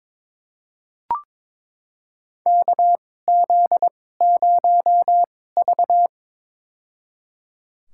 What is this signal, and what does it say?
Morse code, heard as a keyed tone, about 22 wpm, spaced as for 16 wpm, 700 Hz. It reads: KZ0V
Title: USA Call Signs - 4 Characters - 2x Character Spacing - 22wpm